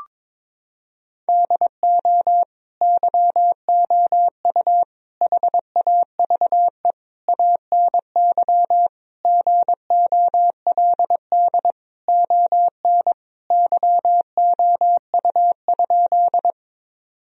Morse code, heard as a keyed tone, about 22 wpm, 700 Hz. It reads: DO YOU HAVE ANY GOLD ON YOU?
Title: Sentences from Top 1000 Words - Review Part 3 - 22wpm